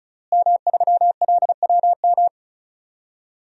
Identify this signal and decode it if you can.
Morse code, heard as a keyed tone, about 36 wpm, 700 Hz. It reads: M3LWM
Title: DXCC-100 Call Signs - 5 Characters - Repeated 2x - 35wpm